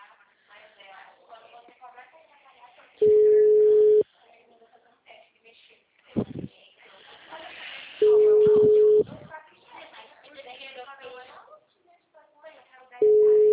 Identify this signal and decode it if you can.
Ringback tone (Europe, 425 Hz), with speech.